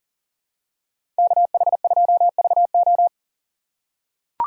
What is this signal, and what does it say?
Morse code, heard as a keyed tone, about 40 wpm, 700 Hz. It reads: KH2VO
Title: USA Call Signs - 4x Character Spacing - 40wpm